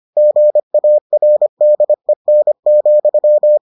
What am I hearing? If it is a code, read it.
Morse code, heard as a keyed tone, about 25 wpm, 600 Hz. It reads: GARDEN,